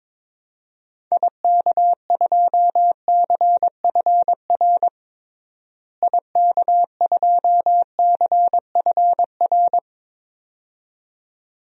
Morse code, heard as a keyed tone, about 22 wpm, 700 Hz. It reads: IK2CFR IK2CFR
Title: DXCC-100 Call Signs - 6 Characters - Repeated 2x - 22wpm